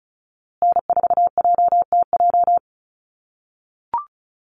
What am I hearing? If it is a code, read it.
Morse code, heard as a keyed tone, about 34 wpm, 700 Hz. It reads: N4JTJ
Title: USA Call Signs Repeated 3x 35wpm